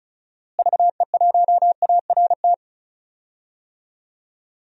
Morse code, heard as a keyed tone, about 36 wpm, 700 Hz. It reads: VE1ART
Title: DXCC-100 Call Signs - 6 Characters - Repeated 3x - 35wpm